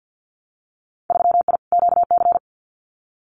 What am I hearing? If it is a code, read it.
Morse code, heard as a keyed tone, about 50 wpm, 700 Hz. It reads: 3I QC
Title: Winter Field Day Exchanges Encore 50wpm